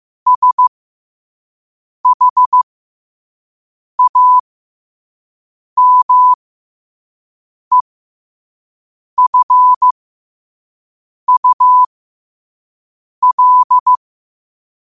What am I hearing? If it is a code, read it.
Morse code, heard as a keyed tone, about 15 wpm, spaced as for 5 wpm, 1000 Hz. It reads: SHAMEFUL
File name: qotdfiles / 5wpm